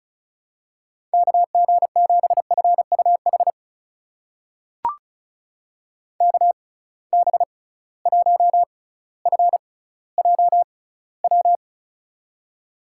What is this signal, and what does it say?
Morse code, heard as a keyed tone, about 36 wpm, spaced as for 18 wpm, 690 Hz. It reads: KG7FUH KB1FJW
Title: USA Call Signs - 6 Characters - 6x Character Spacing - 35wpm